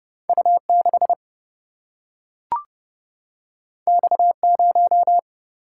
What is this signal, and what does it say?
Morse code, heard as a keyed tone, about 31 wpm, 700 Hz. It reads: U6 X0
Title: Single-Letter Single-Number 30wpm